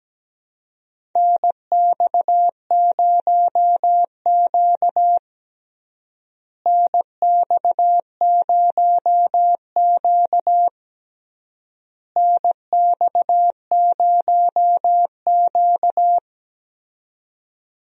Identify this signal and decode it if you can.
Morse code, heard as a keyed tone, about 17 wpm, 700 Hz. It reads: NX0Q NX0Q NX0Q
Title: USA Call Signs - 4 Characters - Repeated 3x - 17wpm